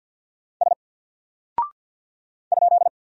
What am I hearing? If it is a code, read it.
Morse code, heard as a keyed tone, about 48 wpm, 700 Hz. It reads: S ?
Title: Single Character (Letter:Number:Punctuation:Prosign) Alt 50wpm